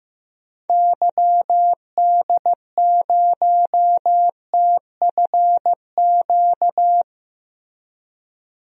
Morse code, heard as a keyed tone, about 15 wpm, 700 Hz. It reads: YD0TFQ
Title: DXCC-100 Call Signs - 6 Characters - Repeated 3x - 15wpm